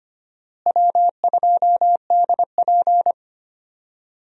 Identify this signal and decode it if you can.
Morse code, heard as a keyed tone, about 25 wpm, 700 Hz. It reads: W2DP
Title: USA Call Signs - 4 Characters - Repeated 3x - 25wpm